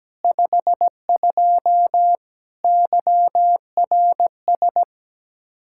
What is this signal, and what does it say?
Morse code, heard as a keyed tone, about 17 wpm, 700 Hz. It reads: 52 YRS
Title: Practice QSO - Exchange 3 - More Info - 17wpm